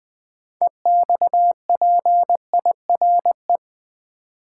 Morse code, heard as a keyed tone, about 20 wpm, 700 Hz. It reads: EXPIRE
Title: 6 Letter Words - First 3 Letters Given - 20wpm